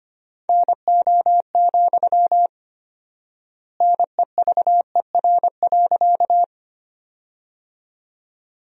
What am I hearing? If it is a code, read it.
Morse code, heard as a keyed tone, about 25 wpm, 700 Hz. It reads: NO, NEVER.
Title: 4x Word-Spacing - Sentences from Top 300 Words - 25wpm